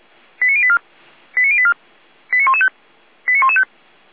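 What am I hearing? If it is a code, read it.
Five-tone selective call (ZVEI), heard as quick runs of five tones, tones of about 70 ms. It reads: 89084 89084 89115 89115
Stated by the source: ZVEI five-tone selective call received with an SDR, hosted by Feuerwehr Orlamünde.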